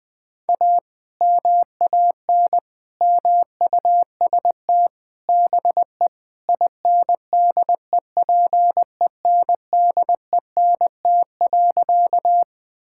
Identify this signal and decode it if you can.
Morse code, heard as a keyed tone, about 20 wpm, 700 Hz. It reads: A MAN MUST BE INDEPENDENT.
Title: Sentences from Top 1000 Words - Review Part 6 - 20wpm